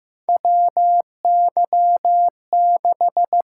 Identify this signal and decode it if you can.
Morse code, heard as a keyed tone, about 15 wpm, 700 Hz. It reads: WY6
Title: USA Call Signs - 4 Characters - 6x Character Spacing - 15wpm